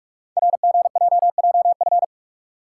Morse code, heard as a keyed tone, about 44 wpm, 700 Hz. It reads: RGJJF
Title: Code Groups - 5 Letters - 45wpm